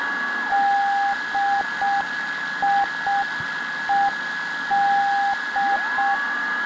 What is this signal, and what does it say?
Morse code, heard as a keyed tone, about 6 wpm, 800 Hz, with noise. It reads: DIED